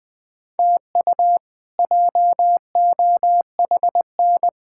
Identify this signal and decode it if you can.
Morse code, heard as a keyed tone, about 20 wpm, 700 Hz. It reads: TU JOHN
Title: K1USN SST QSO Practice - 20wpm